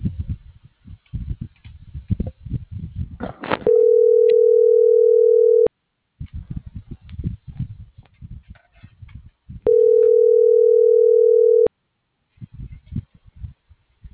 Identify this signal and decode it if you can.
Ringback tone (North America), 440 + 480 Hz.